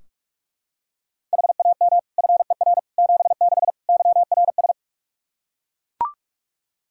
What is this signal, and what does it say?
Morse code, heard as a keyed tone, about 41 wpm, 700 Hz. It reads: HAM FER 76 YRS
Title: K1USN SST QSO Practice 45wpm